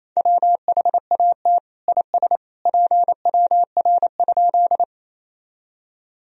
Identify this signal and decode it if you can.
Morse code, heard as a keyed tone, about 28 wpm, 700 Hz. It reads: WHAT IS PWR?
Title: Sentences from Top 500 Words and Abbreviations 28wpm